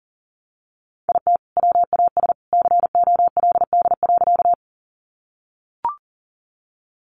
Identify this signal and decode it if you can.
Morse code, heard as a keyed tone, about 40 wpm, 700 Hz. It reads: IT WAS COLD.
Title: Sentences from Top 400 Words 40wpm